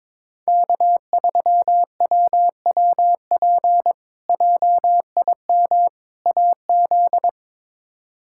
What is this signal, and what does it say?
Morse code, heard as a keyed tone, about 22 wpm, 700 Hz. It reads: K3WWP JIM AZ
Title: K1USN SST QSO Practice - 22wpm